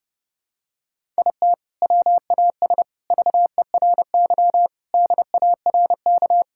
Morse code, heard as a keyed tone, about 30 wpm, 700 Hz. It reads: IT WAS VERY DARK
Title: Sentences from Top 1000 Words - Review Part 6 - 30wpm